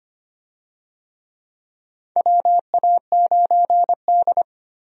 Morse code, heard as a keyed tone, about 25 wpm, 700 Hz. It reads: WA9D
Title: USA Call Signs - 4 Characters - 25wpm